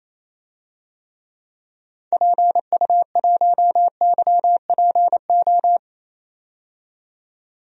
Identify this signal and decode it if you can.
Morse code, heard as a keyed tone, about 28 wpm, 700 Hz. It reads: PU1YPO
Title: DXCC-100 Call Signs - 6 Characters - 28wpm